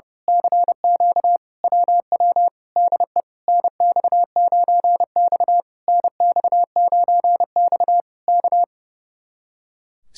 Morse code, heard as a keyed tone, about 29 wpm, 700 Hz. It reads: CQ WW DE NX9X NX9X K
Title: Speed-Racing Common QSO Phrases - 30wpm